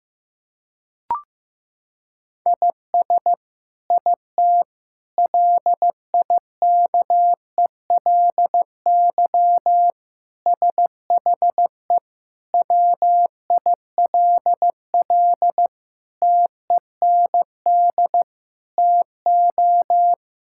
Morse code, heard as a keyed tone, about 15 wpm, 700 Hz. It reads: IS IT LIKELY SHE WILL TEND TO
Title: Sentences from Top 1000 Words - Review Part 5 - 15wpm